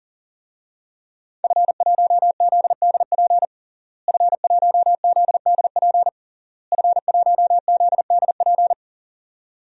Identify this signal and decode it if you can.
Morse code, heard as a keyed tone, about 40 wpm, 700 Hz. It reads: F1ZDP F1ZDP F1ZDP
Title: DXCC-100 Call Signs - 5 Characters - Repeated 3x - 40wpm